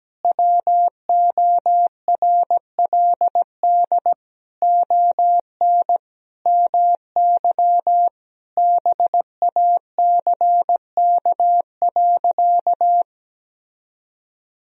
Morse code, heard as a keyed tone, about 17 wpm, 700 Hz. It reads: WORLD ON MY BACK.
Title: Sentences from Top 1000 Words - Review Part 1 - 17wpm